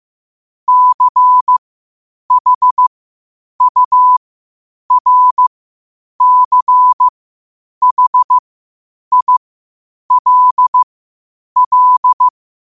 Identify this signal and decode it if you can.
Morse code, heard as a keyed tone, about 15 wpm, spaced as for 8 wpm, 1000 Hz. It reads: CHURCHILL
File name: qotdfiles / 8wpm